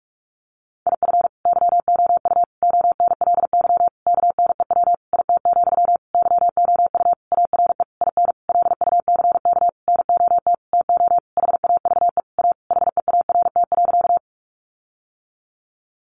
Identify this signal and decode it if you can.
Morse code, heard as a keyed tone, about 44 wpm, 700 Hz. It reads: IF YOU ONLY KNEW IT, YOU ARE IN LUCK NOT TO HAVE A HEART.
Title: Sentences from Top 1500 Words 45wpm